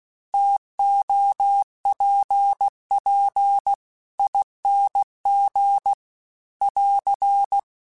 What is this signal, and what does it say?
Morse code, heard as a keyed tone, about 16 wpm, 800 Hz. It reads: TOPP ING <AR>